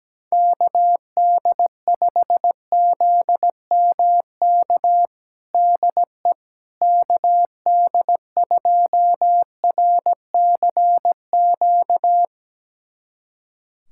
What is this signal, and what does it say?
Morse code, heard as a keyed tone, about 17 wpm, 700 Hz. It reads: KD5ZMK DE KD2RCQ
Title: USA Callsign DE Callsign - 6 Characters - 17wpm